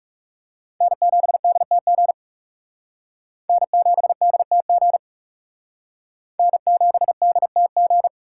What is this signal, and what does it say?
Morse code, heard as a keyed tone, about 37 wpm, 700 Hz. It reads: N7DTG N7DTG N7DTG
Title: Speed Racing - USA Call Signs 30wpm